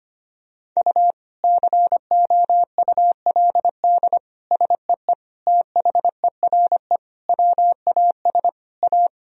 Morse code, heard as a keyed tone, about 25 wpm, 700 Hz. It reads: U COULD SEE THERE WAS A
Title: Sentences from Top 500 Words and Abbreviations 25wpm